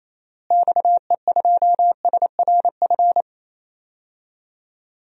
Morse code, heard as a keyed tone, about 28 wpm, 700 Hz. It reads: XE2SRF
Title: DXCC-100 Call Signs - 6 Characters - Repeated 3x - 28wpm